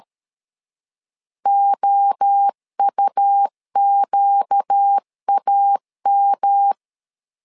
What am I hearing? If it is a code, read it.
Morse code, heard as a keyed tone, about 13 wpm, 800 Hz. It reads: OUQAM
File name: LEZIONI / 4 Q 12wpm 1833